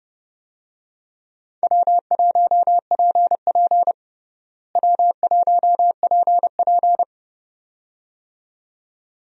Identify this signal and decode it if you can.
Morse code, heard as a keyed tone, about 30 wpm, 700 Hz. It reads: W1PP W1PP
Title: USA Call Signs - 4 Characters - Repeated 2x - 30wpm